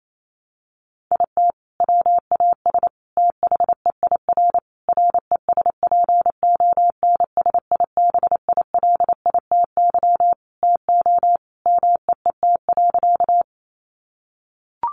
Morse code, heard as a keyed tone, about 28 wpm, 700 Hz. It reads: IT WAS THEIR RESPONSIBILITY TO MEET.
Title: Sentences from Top 1200 Words 28wpm